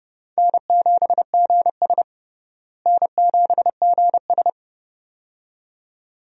Morse code, heard as a keyed tone, about 30 wpm, 700 Hz. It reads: N7GS N7GS
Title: USA Call Signs - 4 Characters - Repeated 2x - 30wpm